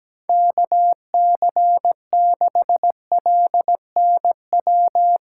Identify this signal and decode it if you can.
Morse code, heard as a keyed tone, about 17 wpm, 700 Hz. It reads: KC6LNW
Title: USA Call Signs - 6 Characters - 4x Character Spacing - 17wpm